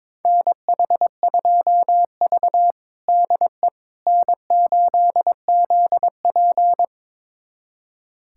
Morse code, heard as a keyed tone, about 22 wpm, 700 Hz. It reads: NH2V DE N8ZP
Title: USA Callsign DE Callsign - 4 Characters - 22wpm